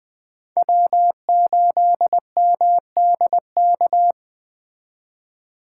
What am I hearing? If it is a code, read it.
Morse code, heard as a keyed tone, about 21 wpm, 700 Hz. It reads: W8MDK
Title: Speed Racing - USA 1x3 Call Signs 17wpm